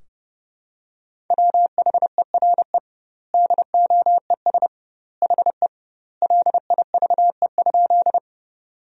Morse code, heard as a keyed tone, about 30 wpm, 700 Hz. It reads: WHERE DOES HE LIVE?
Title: 2x Word-Spacing - Sentences from Top 200 Words - 30wpm